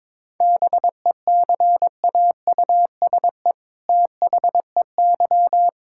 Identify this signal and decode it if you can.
Morse code, heard as a keyed tone, about 22 wpm, 700 Hz. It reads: BECAUSE THEY
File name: All-Age Appropriate Jokes - 22wpm